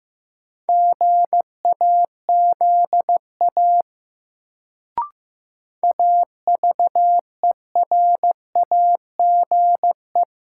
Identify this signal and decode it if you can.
Morse code, heard as a keyed tone, about 15 wpm, 700 Hz. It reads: GAZA AVERAGE
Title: Sets of 5 Words - New Word Drill - 15wpm